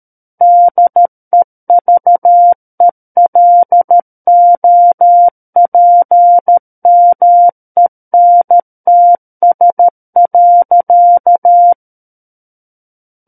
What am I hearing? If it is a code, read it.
Morse code, heard as a keyed tone, about 13 wpm, 700 Hz. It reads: DEVELOPMENTS.
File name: Book One / 04-Chapter 4 - 13 WPM